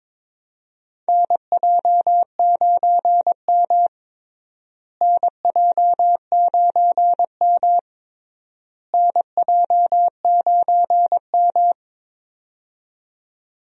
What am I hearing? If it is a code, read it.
Morse code, heard as a keyed tone, about 22 wpm, 700 Hz. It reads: NJ9M NJ9M NJ9M
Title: USA Call Signs - 4 Characters - Repeated 3x - 22wpm